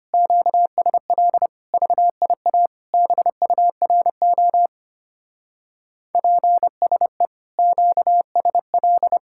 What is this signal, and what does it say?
Morse code, heard as a keyed tone, about 32 wpm, 700 Hz. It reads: QSL VIA BURO PSE QSL